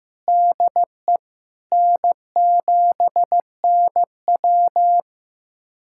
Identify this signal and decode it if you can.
Morse code, heard as a keyed tone, about 16 wpm, 700 Hz. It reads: DE N7NW
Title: USA Callsign DE Callsign - 4 Characters - 15wpm